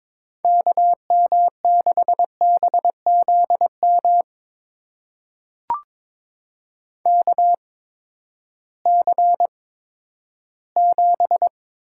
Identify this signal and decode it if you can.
Morse code, heard as a keyed tone, about 22 wpm, spaced as for 12 wpm, 700 Hz. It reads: KM6BZM KC7
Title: USA Call Signs - 6 Characters - 8x Character Spacing - 22wpm